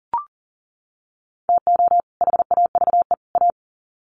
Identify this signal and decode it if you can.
Morse code, heard as a keyed tone, about 40 wpm, 700 Hz. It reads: TO HAVE A
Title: Sets of 3 Words 40wpm